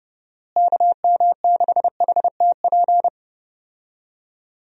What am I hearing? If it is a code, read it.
Morse code, heard as a keyed tone, about 29 wpm, 700 Hz. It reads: KM6HTP
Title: USA Call Signs - 6 Characters - Repeated 2x - 30wpm